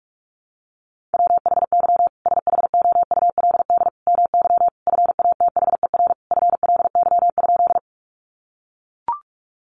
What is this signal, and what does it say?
Morse code, heard as a keyed tone, about 44 wpm, 700 Hz. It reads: WHY SHOULD MY FATHER FLY?
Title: Sentences from Top 500 Words 45wpm